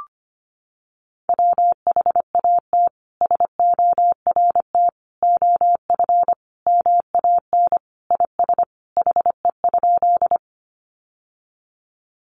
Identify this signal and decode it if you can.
Morse code, heard as a keyed tone, about 25 wpm, 700 Hz. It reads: WHAT SORT OF MAN IS HE?
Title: Sentences from Top 1300 Words 25wpm